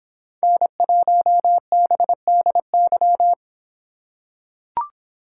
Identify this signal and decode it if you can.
Morse code, heard as a keyed tone, about 26 wpm, 700 Hz. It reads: N1BDY